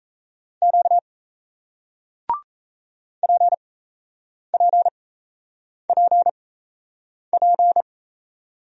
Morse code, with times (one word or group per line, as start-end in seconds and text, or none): 0.6-1.0 Q
3.2-7.8 PPPP